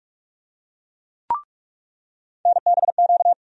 Morse code, heard as a keyed tone, about 46 wpm, 700 Hz. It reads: NBQ
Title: USA Call Sign Postfixes - 3 Letters 45wpm